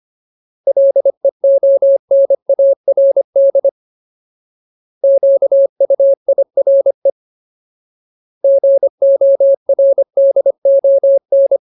Morse code, heard as a keyed tone, about 25 wpm, 550 Hz. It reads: LEONARD QUIRE GORDON
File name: Ultimate Word List / 25WPM Ultimate